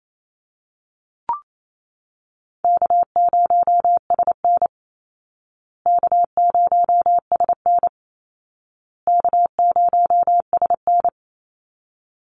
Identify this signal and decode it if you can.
Morse code, heard as a keyed tone, about 28 wpm, 700 Hz. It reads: K0SN K0SN K0SN
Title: Easy Call Signs Repeated 3x - 28wpm